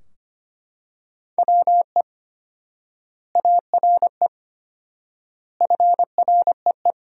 Morse code, heard as a keyed tone, about 25 wpm, 700 Hz. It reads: WE ARE FREE
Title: 4x Word-Spacing - Sentences from Top 500 Words - 25wpm